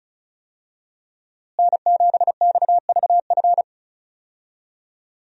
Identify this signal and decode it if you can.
Morse code, heard as a keyed tone, about 34 wpm, 700 Hz. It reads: N7XVF
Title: USA Call Signs - 5 Characters - Repeated 2x - 35wpm